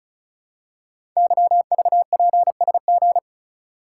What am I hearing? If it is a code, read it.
Morse code, heard as a keyed tone, about 36 wpm, 700 Hz. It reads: YVPSG
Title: Code Groups - 5 Letters - 35wpm